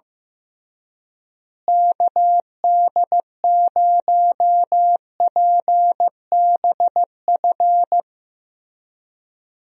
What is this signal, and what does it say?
Morse code, heard as a keyed tone, about 15 wpm, 700 Hz. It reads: KD0PBF
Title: USA Call Signs - 6 Characters - Repeated 3x - 15wpm